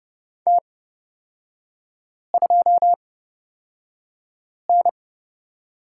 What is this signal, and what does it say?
Morse code, heard as a keyed tone, about 31 wpm, spaced as for 5 wpm, 700 Hz. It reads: T2N
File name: Lesson 38e - Letter q - Three Characters 30wpm